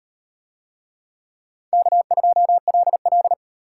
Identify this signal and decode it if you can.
Morse code, heard as a keyed tone, about 32 wpm, 700 Hz. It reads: K2LL